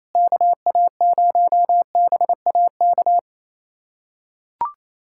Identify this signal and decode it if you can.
Morse code, heard as a keyed tone, about 28 wpm, 700 Hz. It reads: KA0BAK
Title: USA Callsign DE Callsign 28wpm